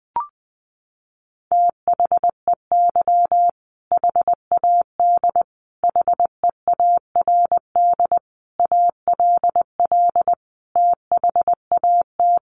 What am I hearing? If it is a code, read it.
Morse code, heard as a keyed tone, about 20 wpm, 700 Hz. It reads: THEY HAD HEARD ALL THAT
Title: Sentences from Top 400 Words 20wpm